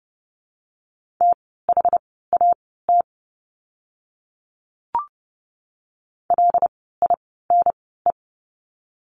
Morse code, heard as a keyed tone, about 30 wpm, spaced as for 17 wpm, 700 Hz. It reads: THAT LINE
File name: course / 30:10wpm / 115 Lesson 19f - Letter w - Words 30wpm